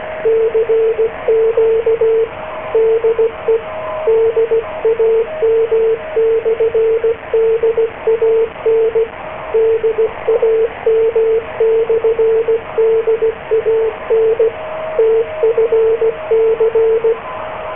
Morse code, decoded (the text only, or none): CQ DE DAM/DAN DAM/DAN TFC